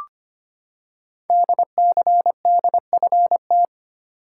Morse code, heard as a keyed tone, about 25 wpm, 700 Hz. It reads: DCDFT
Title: Code Groups - 5 Letters - 25wpm